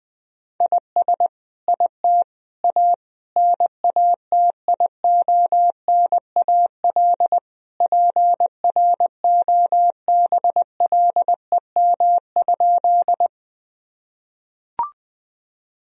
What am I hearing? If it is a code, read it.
Morse code, heard as a keyed tone, about 21 wpm, 700 Hz. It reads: IS IT A NATIONAL PROBLEM?
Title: Sentences from Top 1000 Words - Review Part 4 - 20wpm